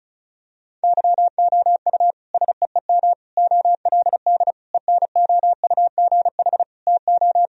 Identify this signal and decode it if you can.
Morse code, heard as a keyed tone, about 36 wpm, 700 Hz. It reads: YOU SEEM OLD ENOUGH TO
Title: Sentences from Top 1000 Words - Review Part 1 - 35wpm